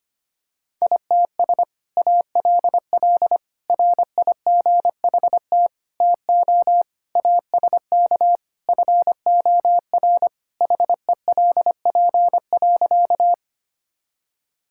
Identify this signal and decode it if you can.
Morse code, heard as a keyed tone, about 25 wpm, 700 Hz. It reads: ITS ALL RIGHT TO ASK FOR HELP.
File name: practice / 25wpm / Sentences with 2-Word Phrases - 25wpm